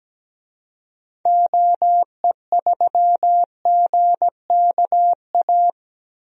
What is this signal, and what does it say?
Morse code, heard as a keyed tone, about 17 wpm, 700 Hz. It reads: OE3GKA
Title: DXCC-100 Call Signs - 6 Characters - Repeated 2x - 17wpm